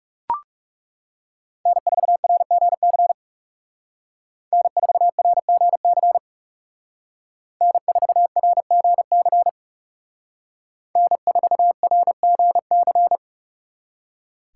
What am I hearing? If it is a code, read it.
Morse code, heard as a keyed tone, about 38 wpm, 700 Hz. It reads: N4RGC N4RGC N4RGC N4RGC